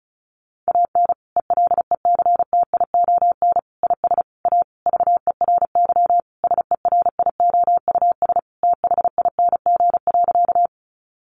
Morse code, heard as a keyed tone, about 34 wpm, 700 Hz. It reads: AN ELECTION IS A VERY SERIOUS THING.
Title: Sentences from Top 1500 Words 35wpm